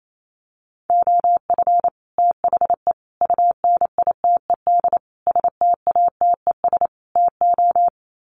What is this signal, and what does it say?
Morse code, heard as a keyed tone, about 28 wpm, 700 Hz. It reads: OF THE UNITED STATES TO
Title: Sets of 5 Words Encore 28wpm